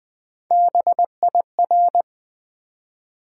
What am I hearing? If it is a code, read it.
Morse code, heard as a keyed tone, about 20 wpm, 700 Hz. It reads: BIR
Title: Build A Word - 1 Letter At A Time - 4-Letter Words - 20wpm